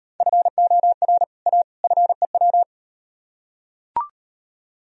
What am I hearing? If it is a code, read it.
Morse code, heard as a keyed tone, about 38 wpm, 700 Hz. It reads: FOR A FEW